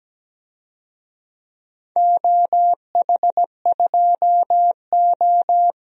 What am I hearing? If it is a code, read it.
Morse code, heard as a keyed tone, about 17 wpm, 700 Hz. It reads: OH2O
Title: DXCC-100 Call Signs - 4 Characters - 17wpm